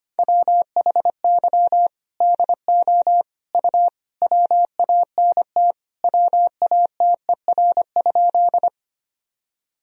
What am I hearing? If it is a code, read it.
Morse code, heard as a keyed tone, about 25 wpm, 700 Hz. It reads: WHY DO U WANT WATER?
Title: Sentences from Top 200 Words and Abbreviations 25wpm